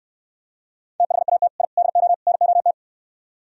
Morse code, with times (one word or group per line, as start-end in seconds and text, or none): 1.0-1.5 THAT
1.6-1.6 I
1.8-2.1 DO
2.3-2.7 NOT